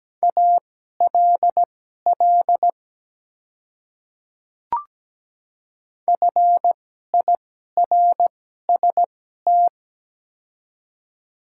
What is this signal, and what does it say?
Morse code, heard as a keyed tone, about 17 wpm, spaced as for 12 wpm, 690 Hz. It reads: ALL FIRST
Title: 2x Character Spacing - Top 100 Words - 17wpm